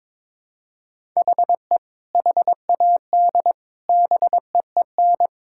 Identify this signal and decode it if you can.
Morse code, heard as a keyed tone, about 21 wpm, 700 Hz. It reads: HE HAD BEEN
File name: practice / 22wpm / Sentences with 3-Word Phrases - 22wpm